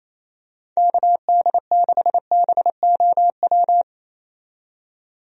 Morse code, heard as a keyed tone, about 28 wpm, 700 Hz. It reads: KD6BOW
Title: USA Call Signs - 6 Characters - 28wpm